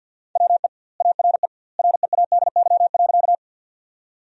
Morse code, heard as a keyed tone, about 50 wpm, 700 Hz. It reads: WE ARE READY.
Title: 2x Word-Spacing - Sentences from Top 400 Words - 50wpm